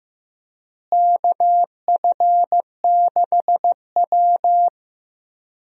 Morse code, heard as a keyed tone, about 15 wpm, 700 Hz. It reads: KF6W
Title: USA Call Signs - 4 Characters - Repeated 2x - 15wpm